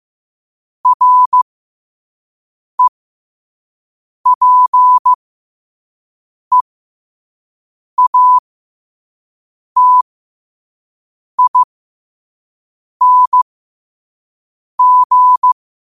Morse code, heard as a keyed tone, about 15 wpm, spaced as for 5 wpm, 1000 Hz. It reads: REPEATING